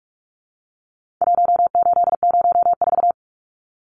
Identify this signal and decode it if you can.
Morse code, heard as a keyed tone, about 46 wpm, 700 Hz. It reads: 1804
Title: Numbers - Contest Style - 4-Digit 45wpm